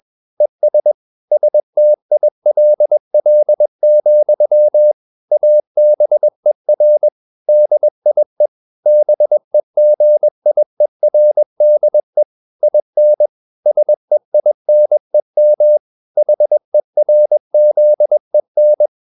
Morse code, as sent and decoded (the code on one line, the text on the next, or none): . ... / ... - .. .-.. .-.. --..-- / .- -... .-. / -.. .. / -... --. .. .-. -.. / .. -. / ... .. -. -- / .... .-. --.. -.
ES STILL, ABER DIE BEGIERDE IN SEINEM HERZEN